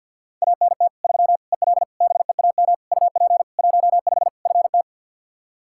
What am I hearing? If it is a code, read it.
Morse code, heard as a keyed tone, about 46 wpm, 700 Hz. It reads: ANT 3 EL BEAM UP 15 FT
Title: K1USN SST QSO Practice 50wpm